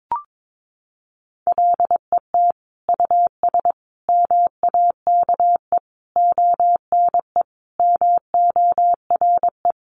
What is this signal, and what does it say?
Morse code, heard as a keyed tone, about 22 wpm, 700 Hz. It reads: LET US MAKE ONE MORE
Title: Sentences from Top 700 Words 22wpm